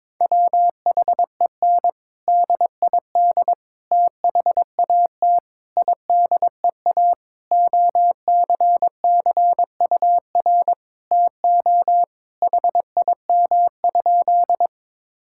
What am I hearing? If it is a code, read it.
Morse code, heard as a keyed tone, about 21 wpm, 700 Hz. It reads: WHEN DID THAT IDEA OCCUR TO HIM?
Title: Sentences from Top 1000 Words - Review Part 6 - 22wpm